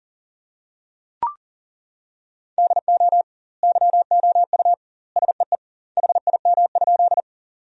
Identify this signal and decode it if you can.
Morse code, heard as a keyed tone, about 40 wpm, 700 Hz. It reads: DO YOU SEE HIM?
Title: 2x Word-Spacing - Sentences from Top 100 Words - 40wpm